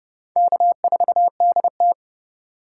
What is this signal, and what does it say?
Morse code, heard as a keyed tone, about 36 wpm, 700 Hz. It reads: K4DT
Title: Speed Racing - USA 1x2 Call Signs 30wpm